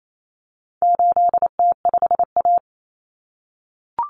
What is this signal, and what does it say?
Morse code, heard as a keyed tone, about 28 wpm, 700 Hz. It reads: 8T5A
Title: Common Cut Numbers - 4-Digit 28wpm